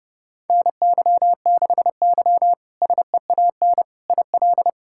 Morse code, heard as a keyed tone, about 30 wpm, 700 Hz. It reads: NY6Y SEAN IL
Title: K1USN SST QSO Practice - 30wpm